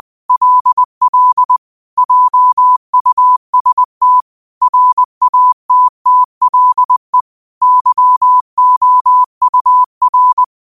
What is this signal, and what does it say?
Morse code, heard as a keyed tone, about 20 wpm, 1000 Hz. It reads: LL JUST RATTLE YOUR
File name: qotdfiles / 20wpm